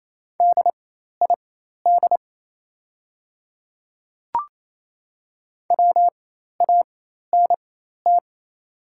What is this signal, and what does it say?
Morse code, heard as a keyed tone, about 28 wpm, spaced as for 13 wpm, 700 Hz. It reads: DID WANT
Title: 4x Character Spacing - Top 200 Words - 28wpm